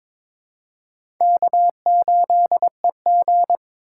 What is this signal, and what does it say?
Morse code, heard as a keyed tone, about 22 wpm, 700 Hz. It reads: K8EG
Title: USA Call Signs - 4 Characters - 6x Character Spacing - 22wpm